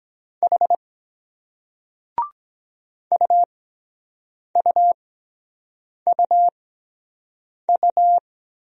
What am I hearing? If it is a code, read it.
Morse code, heard as a keyed tone, about 23 wpm, spaced as for 7 wpm, 700 Hz. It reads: H UUUU